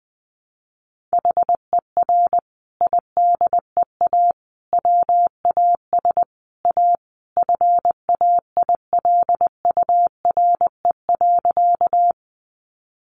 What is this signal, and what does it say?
Morse code, heard as a keyed tone, about 20 wpm, 700 Hz. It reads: HER IDEA WAS A FAILURE.
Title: Sentences from Top 900 Words 20wpm